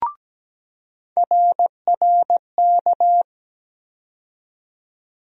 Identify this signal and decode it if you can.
Morse code, heard as a keyed tone, about 17 wpm, 700 Hz. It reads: RRK
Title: USA Call Sign Postfixes - 3 Letters 17wpm